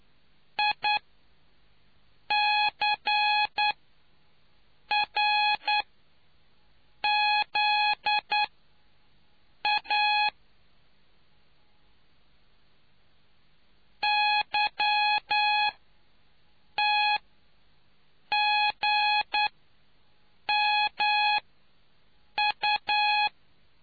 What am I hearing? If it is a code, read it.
Morse code, heard as a keyed tone, about 9 wpm, spaced as for 5 wpm, 840 Hz. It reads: ICRZA YTGMU